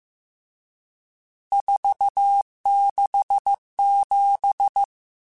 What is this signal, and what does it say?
Morse code, heard as a keyed tone, about 15 wpm, 800 Hz. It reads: 467
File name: RSTs / RST-mp3s / Various RST @13